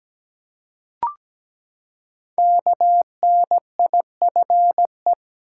Morse code, heard as a keyed tone, about 17 wpm, 700 Hz. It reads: KNIFE
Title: Five Letter Words - 17wpm